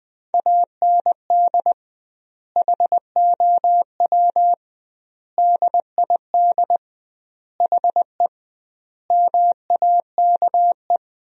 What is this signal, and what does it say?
Morse code, heard as a keyed tone, about 20 wpm, 700 Hz. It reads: AND HOW DID HE MAKE
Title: 2x Word-Spacing - Sentences from Top 400 Words - 20wpm